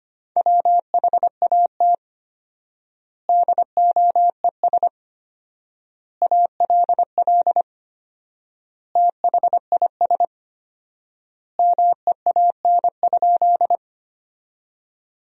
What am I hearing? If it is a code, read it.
Morse code, heard as a keyed tone, about 25 wpm, 700 Hz. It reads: WHAT DOES ALL THIS MEAN?
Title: 4x Word-Spacing - Sentences from Top 200 Words - 25wpm